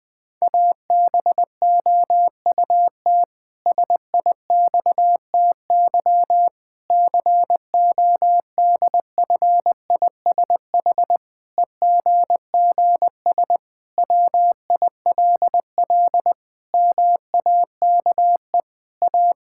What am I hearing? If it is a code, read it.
Morse code, heard as a keyed tone, about 20 wpm, 700 Hz. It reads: ABOUT SIXTY CODFISH EGGS WILL MAKE A
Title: Pangram Challenge v2 - 20wpm